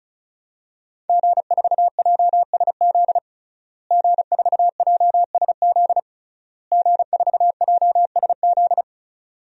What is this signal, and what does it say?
Morse code, heard as a keyed tone, about 36 wpm, 700 Hz. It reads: G4JSZ G4JSZ G4JSZ
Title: DXCC-100 Call Signs - 5 Characters - Repeated 3x - 35wpm